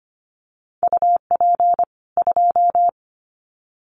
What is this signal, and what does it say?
Morse code, heard as a keyed tone, about 25 wpm, 700 Hz. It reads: UP 2
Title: Common QSO Phrases 25wpm